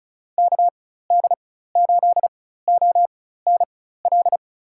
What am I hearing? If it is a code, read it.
Morse code, heard as a keyed tone, about 36 wpm, spaced as for 16 wpm, 700 Hz. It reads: KD8ONL
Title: USA Call Signs - 4x Character Spacing - 35wpm